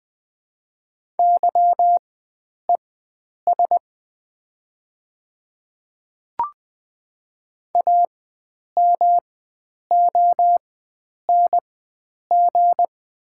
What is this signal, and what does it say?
Morse code, heard as a keyed tone, about 20 wpm, spaced as for 9 wpm, 700 Hz. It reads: YES AMONG
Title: 4x Character Spacing - Top 500 Words - 20wpm